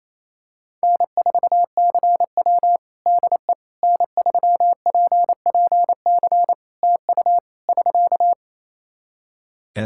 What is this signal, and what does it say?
Morse code, heard as a keyed tone, about 28 wpm, 700 Hz. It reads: N4CW DE N3PPC TU <SK>
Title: Practice QSO - Exchange 4 - Ending - 28wpm